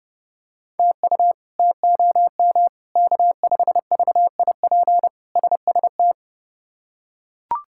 Morse code, heard as a keyed tone, about 30 wpm, 700 Hz. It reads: TU TOM K5VIP SST
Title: K1USN SST QSO Practice - 30wpm